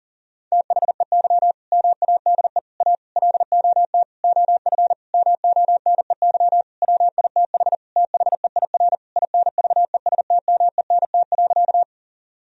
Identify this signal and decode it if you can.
Morse code, heard as a keyed tone, about 40 wpm, 700 Hz. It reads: THEY MADE A LOT OF MONEY WITH THEIR INVESTMENT.
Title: Sentences from Top 1000 Words - Review Part 4 - 40wpm